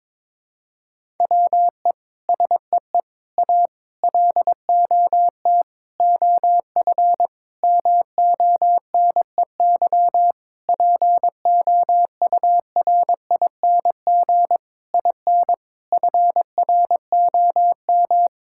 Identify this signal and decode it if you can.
Morse code, heard as a keyed tone, about 21 wpm, 700 Hz. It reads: WE SEE A LOT OF MONEY POURING IN FROM
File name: Sets of 9 Words - 22wpm